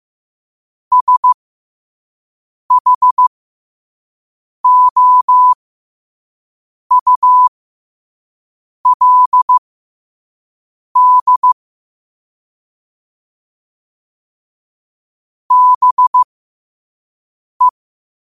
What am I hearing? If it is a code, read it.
Morse code, heard as a keyed tone, about 15 wpm, spaced as for 5 wpm, 1000 Hz. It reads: SHOULD BE
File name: qotdfiles / 5wpm